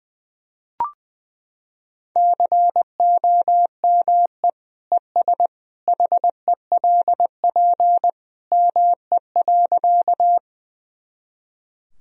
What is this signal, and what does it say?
Morse code, heard as a keyed tone, about 20 wpm, 700 Hz. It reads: COME ES HELP ME.
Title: Sentences from Top 200 Words and Abbreviations 20wpm